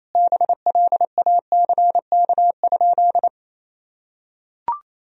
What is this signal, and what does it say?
Morse code, heard as a keyed tone, about 27 wpm, 700 Hz. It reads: BLACK?
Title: Sentences from Top 1000 Words - Review Part 2 - 28wpm